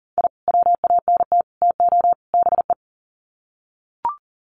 Morse code, heard as a keyed tone, about 40 wpm, 700 Hz. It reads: I WANT TO BE
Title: Sets of 4 Words 40wpm